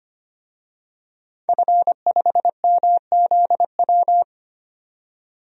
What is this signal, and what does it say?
Morse code, heard as a keyed tone, about 25 wpm, 700 Hz. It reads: F5MZW
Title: DXCC-100 Call Signs - 5 Characters - 25wpm